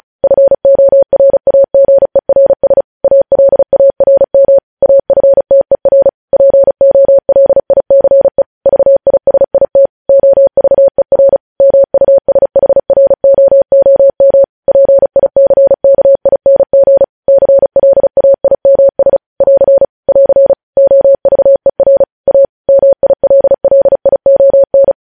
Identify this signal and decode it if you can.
Morse code, heard as a keyed tone, about 36 wpm, 550 Hz. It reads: FORAGERS ALARM AFTER POLICE VISIT OVER MUSHROOM PICKING CLAIMS <AR> <AR> OVER A MILLION
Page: News headlines in Morse code at 35 wpm